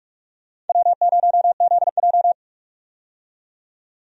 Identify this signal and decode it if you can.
Morse code, heard as a keyed tone, about 44 wpm, 700 Hz. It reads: W0ZJ
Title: USA Call Signs - 4 Characters - 45wpm